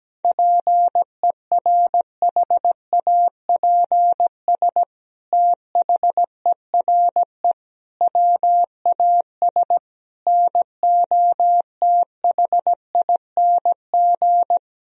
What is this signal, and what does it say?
Morse code, heard as a keyed tone, about 17 wpm, 700 Hz. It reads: PERHAPS THERE WAS NOTHING
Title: Sentences from Top 1000 Words - Review Part 4 - 17wpm